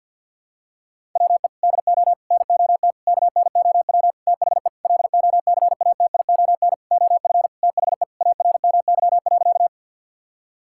Morse code, heard as a keyed tone, about 47 wpm, 700 Hz. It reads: WE DO NOT KNOW THE LOCATION OF THE ARMY.
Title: Sentences from Top 1000 Words - Review Part 6 - 50wpm